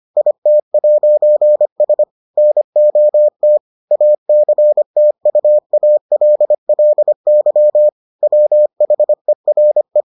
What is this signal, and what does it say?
Morse code, heard as a keyed tone, about 25 wpm, 600 Hz. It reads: IT'S NOT ACTUALLY WHERE